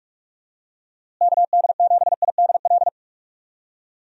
Morse code, heard as a keyed tone, about 46 wpm, 700 Hz. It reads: KD7IDL
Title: USA Call Signs - 6 Characters - Repeated 2x - 45wpm